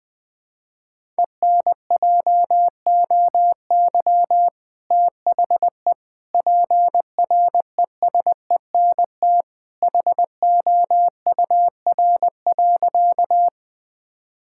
Morse code, heard as a keyed tone, about 21 wpm, 700 Hz. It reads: ENJOY THE PRESENT HOUR.
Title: Sentences from Top 1000 Words - Review Part 5 - 20wpm